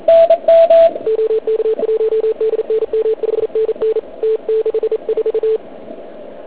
Morse code, heard as a keyed tone, about 36 wpm, 440 Hz. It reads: OK1DNM 5NN T64